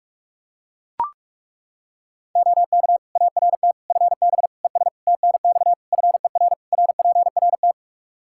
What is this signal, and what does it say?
Morse code, heard as a keyed tone, about 41 wpm, 700 Hz. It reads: OK ART FB ES TNX FER RPRT
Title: Practice QSO - Exchange 2 - Info - 45wpm